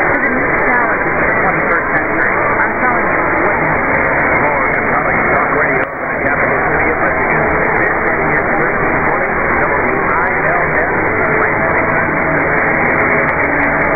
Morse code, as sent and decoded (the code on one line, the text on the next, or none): none